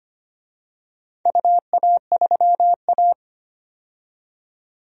Morse code, heard as a keyed tone, about 25 wpm, 700 Hz. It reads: UA3A
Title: DXCC-100 Call Signs - 4 Characters - 25wpm